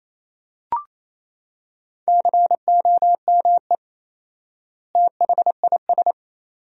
Morse code, heard as a keyed tone, about 28 wpm, 700 Hz. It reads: COME THIS
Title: 4x Word-Spacing - Sentences from Top 100 Words - 28wpm